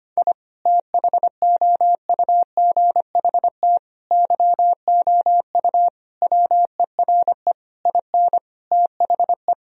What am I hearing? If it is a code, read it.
Morse code, heard as a keyed tone, about 25 wpm, 700 Hz. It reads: I THOUGHT YOU WERE IN THE
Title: Sentences from Top 1000 Words - Review Part 1 - 25wpm